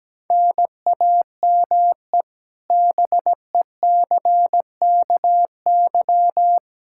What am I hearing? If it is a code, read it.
Morse code, heard as a keyed tone, about 17 wpm, 700 Hz. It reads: NAME BECKY
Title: Practice QSO - Exchange 1 - Beginning - 17wpm